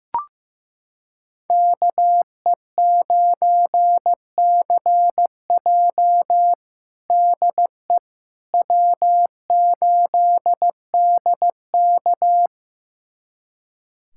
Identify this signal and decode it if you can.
Morse code, heard as a keyed tone, about 15 wpm, 700 Hz. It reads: KE9CJ DE W8DK
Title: USA Callsign DE Callsign 15wpm